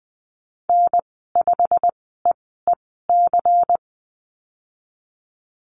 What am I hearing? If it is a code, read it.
Morse code, heard as a keyed tone, about 20 wpm, spaced as for 15 wpm, 700 Hz. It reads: N5EEC
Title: 179 Lesson 29g - Slash - Call Signs 20wpm